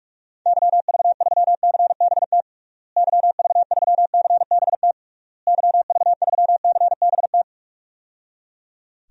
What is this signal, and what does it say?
Morse code, heard as a keyed tone, about 46 wpm, 700 Hz. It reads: YV3CBT YV3CBT YV3CBT
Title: DXCC-100 Call Signs - 6 Characters - Repeated 3x - 45wpm